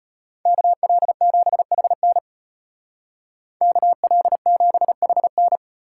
Morse code, heard as a keyed tone, about 33 wpm, 700 Hz. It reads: KL7HN KL7HN